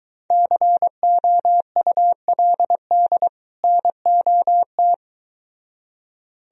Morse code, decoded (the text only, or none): COULD NOT